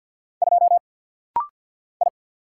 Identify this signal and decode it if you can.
Morse code, heard as a keyed tone, about 48 wpm, 700 Hz. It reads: 2 I
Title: Single Letter-Number Alt - Rapid-Fire 50wpm